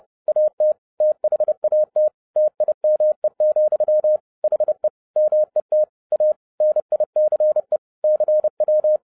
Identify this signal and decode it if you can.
Morse code, heard as a keyed tone, about 30 wpm, 610 Hz. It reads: AT THAT TIME, HE MET A NICE CW
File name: 30wpm